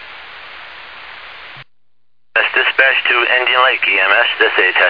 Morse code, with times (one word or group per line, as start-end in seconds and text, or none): none